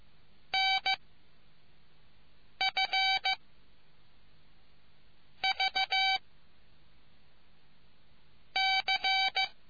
Morse code, heard as a keyed tone, about 15 wpm, 770 Hz. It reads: N F V C